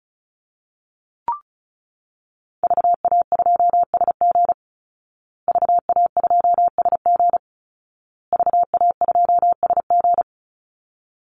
Morse code, heard as a keyed tone, about 34 wpm, 700 Hz. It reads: VA2SG VA2SG VA2SG
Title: Call Signs Repeated 2x - 35wpm